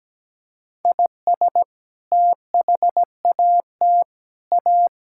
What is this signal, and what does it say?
Morse code, heard as a keyed tone, about 17 wpm, 700 Hz. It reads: IS THAT A
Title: Sentences from Top 1000 Words - Review Part 5 - 17wpm